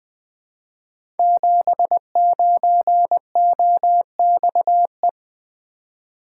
Morse code, heard as a keyed tone, about 20 wpm, 700 Hz. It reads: 79OXE
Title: Code Groups - 5 Letter:Numbers - 20wpm